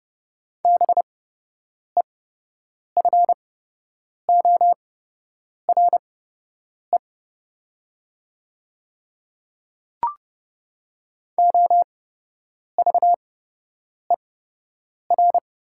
Morse code, heard as a keyed tone, about 30 wpm, spaced as for 8 wpm, 690 Hz. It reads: BEFORE OVER
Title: 8x Character Spacing - Top 200 Words - 30wpm